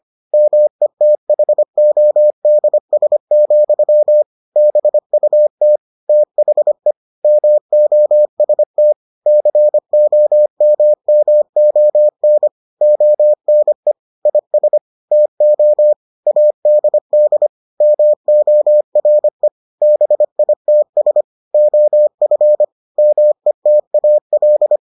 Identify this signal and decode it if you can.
Morse code, heard as a keyed tone, about 25 wpm, 600 Hz. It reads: METHODS, BUT THE MOST COMMON ONE IS TO ADD MORE BITS OF METAL